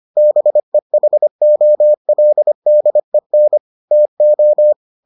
Morse code, heard as a keyed tone, about 25 wpm, 600 Hz. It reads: BEHOLDEN TO